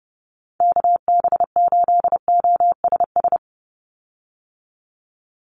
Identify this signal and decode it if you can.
Morse code, heard as a keyed tone, about 30 wpm, 700 Hz. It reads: KB8OSS
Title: USA Call Signs 30wpm